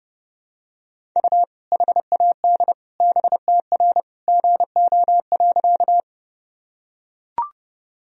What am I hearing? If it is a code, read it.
Morse code, heard as a keyed tone, about 30 wpm, 700 Hz. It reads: U HAD BTR GO.
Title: Sentences from Top 400 Words and Abbreviations 30wpm